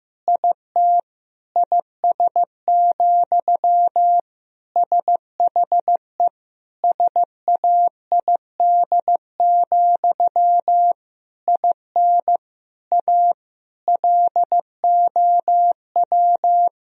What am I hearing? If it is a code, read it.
Morse code, heard as a keyed tone, about 15 wpm, 700 Hz. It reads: IT IS, SHE SAID, IN A LOW
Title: Sentences from Top 1000 Words - Review Part 1 - 15wpm